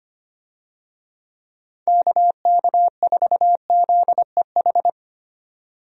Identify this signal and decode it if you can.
Morse code, heard as a keyed tone, about 25 wpm, 700 Hz. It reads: KK4ZEH
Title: USA Call Signs - 6 Characters - 25wpm